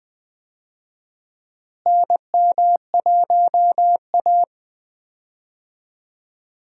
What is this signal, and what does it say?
Morse code, heard as a keyed tone, about 20 wpm, 700 Hz. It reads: NM1A
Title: USA Call Signs - 4 Characters - Repeated 2x - 20wpm